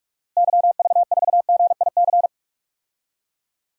Morse code, heard as a keyed tone, about 46 wpm, 700 Hz. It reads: YV4GIC
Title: DXCC-100 Call Signs - 6 Characters - Repeated 3x - 45wpm